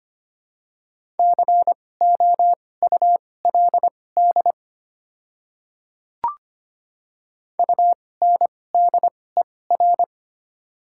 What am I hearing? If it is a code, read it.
Morse code, heard as a keyed tone, about 25 wpm, spaced as for 18 wpm, 700 Hz. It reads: COULD UNDER
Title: 2x Character Spacing - Top 500 Words - 25wpm